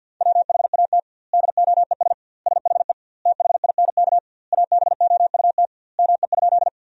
Morse code, heard as a keyed tone, about 50 wpm, 700 Hz. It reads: WHAT DOES SHE THINK ABOUT ME?
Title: 2x Word-Spacing - Sentences from Top 200 Words - 50wpm